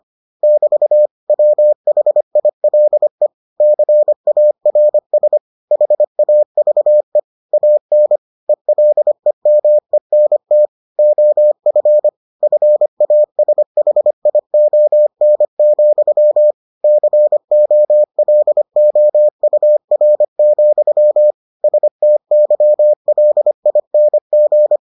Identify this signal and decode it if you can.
Morse code, heard as a keyed tone, about 25 wpm, 600 Hz. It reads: <BT> WHILE CARS HAVE AN ELEMENT OF FASHION, COLOUR, STYLING